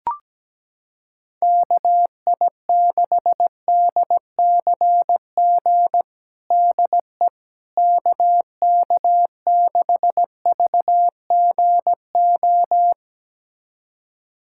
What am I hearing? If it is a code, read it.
Morse code, heard as a keyed tone, about 17 wpm, 700 Hz. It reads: KI6DCG DE KK6VGO
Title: USA Callsign DE Callsign - 6 Characters - 17wpm